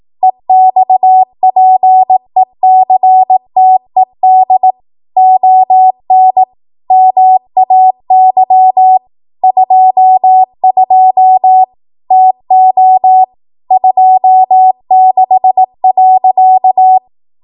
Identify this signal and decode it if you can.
Morse code, heard as a keyed tone, about 18 wpm, 750 Hz. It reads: EXPECTED ON MAY 22 TO 26.
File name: Morse / 18WPM